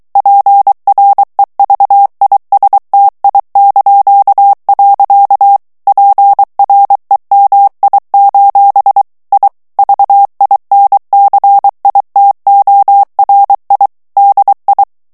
Morse code, heard as a keyed tone, about 23 wpm, 800 Hz. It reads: PREVISTI). PREMI: I VINCITORI DI
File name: ITALIEN texte clair 2 vitesse 22 mots minute